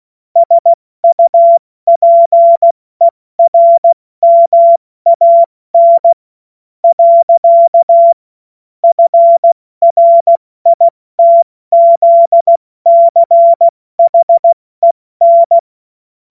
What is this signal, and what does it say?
Morse code, heard as a keyed tone, about 16 wpm, spaced as for 13 wpm, 680 Hz. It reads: SUPERMAN. FRITZCHEN
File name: Morsekurs / Witze / Witz 24. August 2021, 70 F80